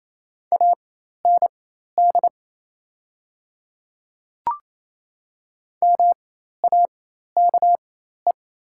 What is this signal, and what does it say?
Morse code, heard as a keyed tone, about 28 wpm, spaced as for 13 wpm, 700 Hz. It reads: AND MAKE
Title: 4x Character Spacing - Top 100 Words - 28wpm